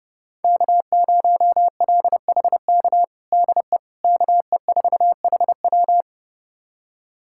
Morse code, heard as a keyed tone, about 32 wpm, 700 Hz. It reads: K0LHK DE KE4HW
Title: USA Callsign DE Callsign - 5 Characters - 30wpm